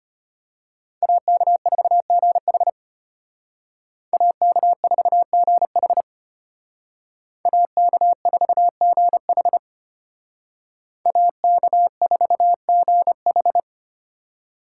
Morse code, heard as a keyed tone, about 30 wpm, 700 Hz. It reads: AK4GH AK4GH AK4GH AK4GH
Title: Speed Racing - USA 2x2 Call Signs 25wpm